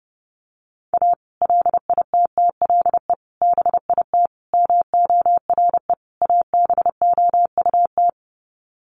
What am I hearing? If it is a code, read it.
Morse code, heard as a keyed tone, about 30 wpm, 700 Hz. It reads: A LITTLE BIT MORE ABOUT
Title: Sets of 5 Words Encore2 30wpm